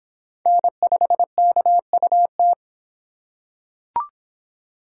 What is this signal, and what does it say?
Morse code, heard as a keyed tone, about 25 wpm, 700 Hz. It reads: N5KUT